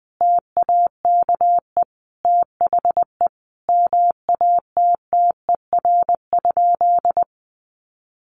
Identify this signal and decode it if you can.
Morse code, heard as a keyed tone, about 20 wpm, 700 Hz. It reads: TAKE THE MATTER?
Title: Sentences from Top 700 Words 20wpm